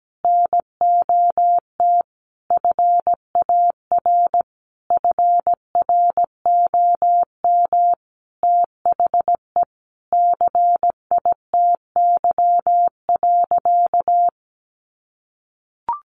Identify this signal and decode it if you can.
Morse code, heard as a keyed tone, about 17 wpm, 700 Hz. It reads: NOT FAR FROM THE CITY.
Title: Sentences from Top 1200 Words 17wpm